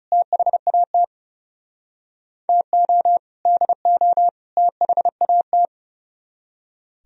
Morse code, heard as a keyed tone, about 39 wpm, 700 Hz. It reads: THAT TO DO THAT